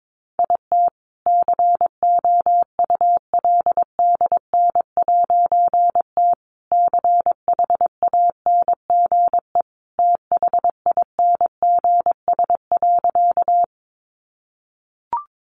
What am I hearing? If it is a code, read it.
Morse code, heard as a keyed tone, about 22 wpm, 700 Hz. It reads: IT COULDN'T CHANGE THINGS.
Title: A Sound Of Thunder 22wpm